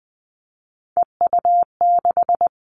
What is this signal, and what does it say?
Morse code, heard as a keyed tone, about 20 wpm, 700 Hz. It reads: EU6
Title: All Cut Numbers - 3-Digit 20wpm